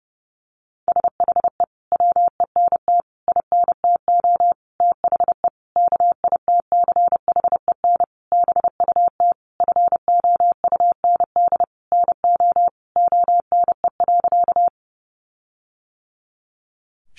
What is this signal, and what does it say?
Morse code, heard as a keyed tone, about 30 wpm, 700 Hz. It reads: SHE WENT INTO THE KITCHEN BUT FOUND NO ONE.
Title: Sentences from Top 1400 Words 30wpm